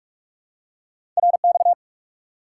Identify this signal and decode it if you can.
Morse code, heard as a keyed tone, about 45 wpm, 700 Hz. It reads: RX
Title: Top 90 QSO Elements 45wpm